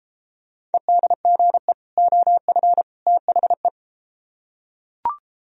Morse code, heard as a keyed tone, about 32 wpm, 700 Hz. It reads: EDGE OF THE